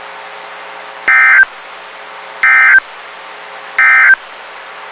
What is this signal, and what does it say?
Emergency Alert System (SAME) tones, heard as harsh buzzing bursts. It is heard from 1.0 to 4.2 s.
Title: WXK91 NOAA Weather Radio